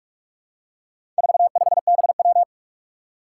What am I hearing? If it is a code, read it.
Morse code, heard as a keyed tone, about 44 wpm, 700 Hz. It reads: 45BW
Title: Four Challenging Letter-Number Combinations 45wpm